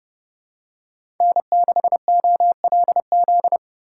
Morse code, heard as a keyed tone, about 28 wpm, 700 Hz. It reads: N6OLZ